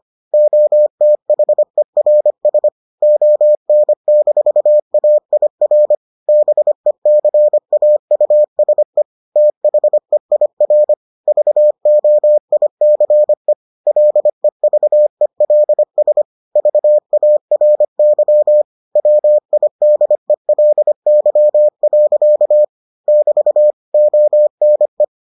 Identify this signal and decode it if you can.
Morse code, heard as a keyed tone, about 25 wpm, 600 Hz. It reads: OTHERS ON-AIR BECAUSE THEIR VOICE LEVELS VARY WIDELY. <BT> ONE